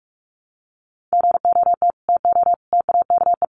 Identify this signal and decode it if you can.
Morse code, heard as a keyed tone, about 44 wpm, 700 Hz. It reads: GOT TO TAKE
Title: Sets of 5 Words Encore 45wpm